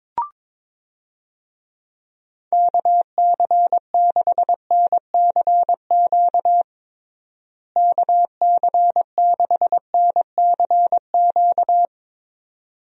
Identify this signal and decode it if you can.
Morse code, heard as a keyed tone, about 22 wpm, 700 Hz. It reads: KC6NCQ KC6NCQ
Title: USA Call Signs - 6 Characters - Repeated 2x - 22wpm